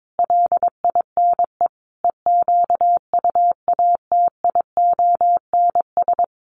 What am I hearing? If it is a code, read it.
Morse code, heard as a keyed tone, about 22 wpm, 700 Hz. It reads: LINE EQUATIONS
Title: Sets of 5 Words 22wpm